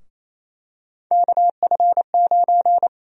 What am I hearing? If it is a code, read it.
Morse code, heard as a keyed tone, about 28 wpm, 700 Hz. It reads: KF9
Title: USA Call Sign Prefixes - Repeated 2x - 28wpm